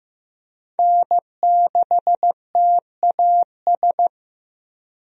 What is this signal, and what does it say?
Morse code, heard as a keyed tone, about 15 wpm, 700 Hz. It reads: N6TAS
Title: USA Call Signs - 5 Characters - Repeated 2x - 15wpm